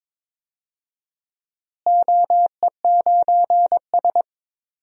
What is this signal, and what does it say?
Morse code, heard as a keyed tone, about 22 wpm, 700 Hz. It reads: OE9S
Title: DXCC-100 Call Signs - 4 Characters - Repeated 2x - 22wpm